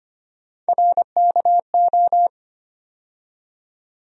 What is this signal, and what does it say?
Morse code, heard as a keyed tone, about 25 wpm, 700 Hz. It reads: RKO
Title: USA Call Sign Postfixes - 3 Letters 25wpm